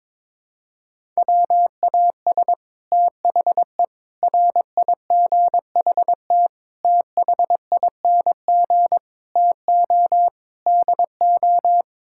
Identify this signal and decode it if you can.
Morse code, heard as a keyed tone, about 22 wpm, 700 Hz. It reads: WAS THE RIGHT THING TO DO
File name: practice / 22wpm / Sets of 6 Words - 22wpm